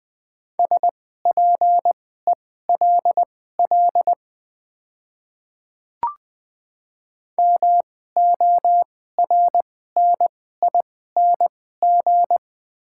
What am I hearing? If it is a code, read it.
Morse code, heard as a keyed tone, about 21 wpm, spaced as for 15 wpm, 690 Hz. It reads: SPELL MORNING
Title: 2x Character Spacing - Top 500 Words - 20wpm